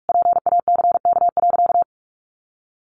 Morse code, heard as a keyed tone, about 46 wpm, 700 Hz. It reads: PACK.
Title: Sentences from Top 1300 Words 45wpm